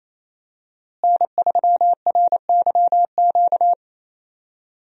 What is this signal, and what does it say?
Morse code, heard as a keyed tone, about 28 wpm, 700 Hz. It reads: N3RYQ
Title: USA Call Signs - 5 Characters - Repeated 2x - 28wpm